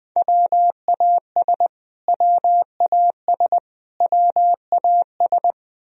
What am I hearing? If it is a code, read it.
Morse code, heard as a keyed tone, about 21 wpm, 700 Hz. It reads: WAS WAS WAS
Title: Top 100 Words Repeated 20wpm